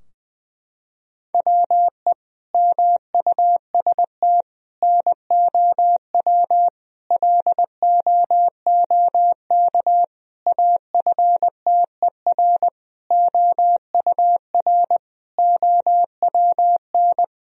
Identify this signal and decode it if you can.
Morse code, heard as a keyed tone, about 21 wpm, 700 Hz. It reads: WE MUST NOW LOOK AFTER OUR OWN
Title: Sentences from Top 1000 Words - Review Part 4 - 20wpm